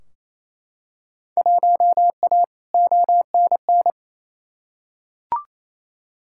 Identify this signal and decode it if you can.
Morse code, heard as a keyed tone, about 28 wpm, 700 Hz. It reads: 1A ONN
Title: Field Day Exchanges Encore - 28wpm